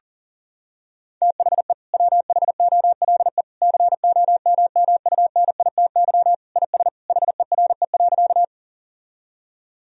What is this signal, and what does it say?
Morse code, heard as a keyed tone, about 40 wpm, 700 Hz. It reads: THE WHOLE COMMUNITY IS HERE.
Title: Sentences from Top 1000 Words - Review Part 4 - 40wpm